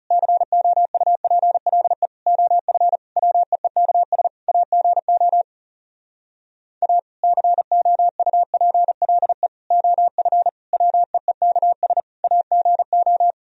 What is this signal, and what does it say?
Morse code, heard as a keyed tone, about 40 wpm, 700 Hz. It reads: COUPLE OF WEEKS AGO A COUPLE OF WEEKS AGO